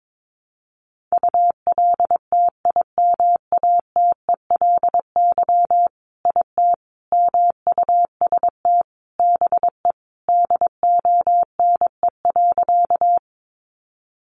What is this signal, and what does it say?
Morse code, heard as a keyed tone, about 22 wpm, 700 Hz. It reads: ULTIMATELY IT MUST BE DONE.
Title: Sentences from Top 1200 Words 22wpm